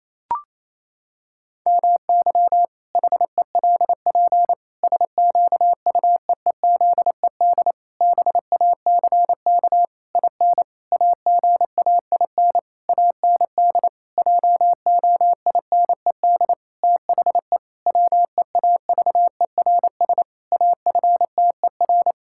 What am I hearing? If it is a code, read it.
Morse code, heard as a keyed tone, about 28 wpm, 700 Hz. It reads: MY HELP SQUEEZED BACK IN AGAIN AND JOINED THE WEAVERS AFTER
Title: Pangram Challenge v2 - 28wpm